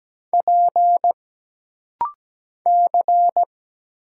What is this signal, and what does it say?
Morse code, heard as a keyed tone, about 17 wpm, 700 Hz. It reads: P C
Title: Single Letter-Number Alt - Rapid-Fire 17wpm